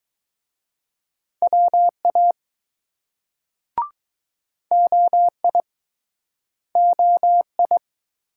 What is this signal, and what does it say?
Morse code, heard as a keyed tone, about 20 wpm, 700 Hz. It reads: WA OI OI